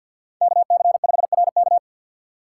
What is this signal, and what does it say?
Morse code, heard as a keyed tone, about 49 wpm, 700 Hz. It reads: KC5RK
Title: USA Call Signs - 5 Characters - 3x Character Spacing - 50wpm